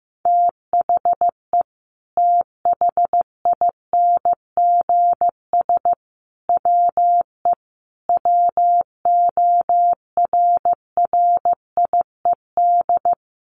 Morse code, heard as a keyed tone, about 15 wpm, 700 Hz. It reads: THE THINGS WE WORRIED
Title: A Sound Of Thunder 15wpm